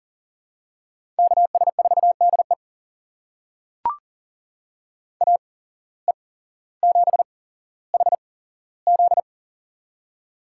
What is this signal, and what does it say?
Morse code, heard as a keyed tone, about 37 wpm, spaced as for 17 wpm, 700 Hz. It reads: KS4DE AE7HZ
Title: USA Call Signs - 5 Characters - 8x Character Spacing - 40wpm